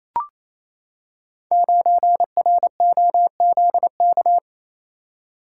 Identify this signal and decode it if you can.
Morse code, heard as a keyed tone, about 27 wpm, 700 Hz. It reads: 9ROZK
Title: Code Groups - 5 Letter:Numbers - 28wpm